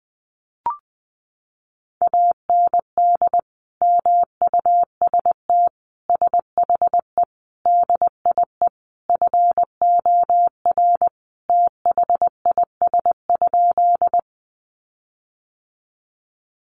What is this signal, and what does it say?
Morse code, heard as a keyed tone, about 20 wpm, 700 Hz. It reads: AND MUST SHE DIE FOR THIS?
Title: Sentences from Top 1300 Words 20wpm